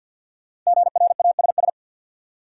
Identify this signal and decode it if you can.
Morse code, heard as a keyed tone, about 48 wpm, 700 Hz. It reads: GRASS
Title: Five Letter Words - 50wpm